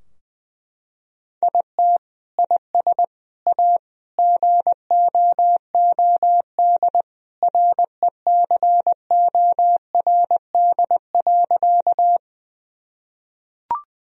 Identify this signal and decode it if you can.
Morse code, heard as a keyed tone, about 20 wpm, 700 Hz. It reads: IT IS A GOOD RECORD.
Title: Sentences from Top 1000 Words - Review Part 3 - 20wpm